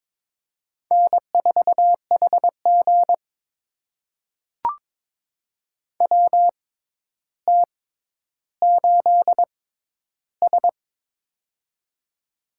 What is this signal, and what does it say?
Morse code, heard as a keyed tone, about 22 wpm, spaced as for 11 wpm, 700 Hz. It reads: N4HG WT8S
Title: USA Call Signs - 4 Characters - 6x Character Spacing - 22wpm